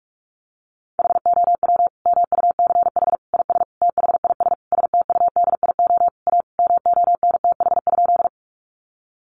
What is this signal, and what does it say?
Morse code, heard as a keyed tone, about 44 wpm, 700 Hz. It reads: HOW MUCH IS THIS STUDIO A MONTH?
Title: Sentences from Top 1200 Words 45wpm